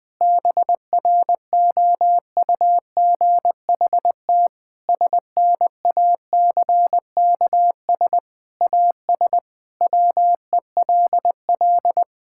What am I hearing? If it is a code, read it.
Morse code, heard as a keyed tone, about 20 wpm, 700 Hz. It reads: BROUGHT SNACKS AS WELL
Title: Sentences with 6-Word Phrases - Spoken Beginning - 20wpm